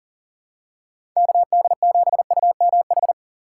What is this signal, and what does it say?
Morse code, heard as a keyed tone, about 40 wpm, 700 Hz. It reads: KD7UMH
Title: USA Call Signs - 6 Characters - 3x Character Spacing - 40wpm